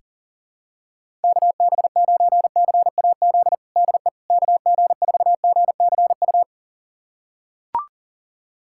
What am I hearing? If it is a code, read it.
Morse code, heard as a keyed tone, about 37 wpm, 700 Hz. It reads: KB9CAZ DE KG4GCU
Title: USA Callsign DE Callsign 40wpm